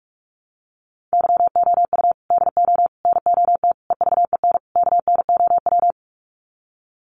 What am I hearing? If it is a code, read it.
Morse code, heard as a keyed tone, about 48 wpm, 700 Hz. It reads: YOU DO NOT EVEN KNOW
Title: Sets of 5 Words Encore2 45wpm